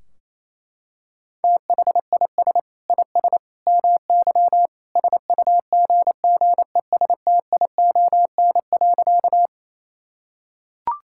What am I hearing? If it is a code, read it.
Morse code, heard as a keyed tone, about 28 wpm, 700 Hz. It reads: THIS IS MY SUGGESTION.
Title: Sentences from Top 1600 Words 28wpm